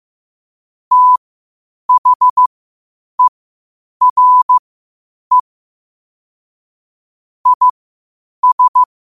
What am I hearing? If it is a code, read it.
Morse code, heard as a keyed tone, about 15 wpm, spaced as for 8 wpm, 1000 Hz. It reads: THERE IS
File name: qotdfiles / 8wpm